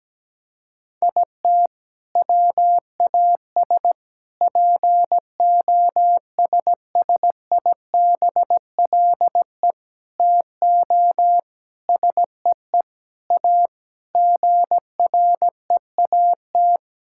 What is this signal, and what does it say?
Morse code, heard as a keyed tone, about 17 wpm, 700 Hz. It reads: IT WAS POSSIBLE TO SEE A GREAT
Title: Sentences from Top 1600 Words 17wpm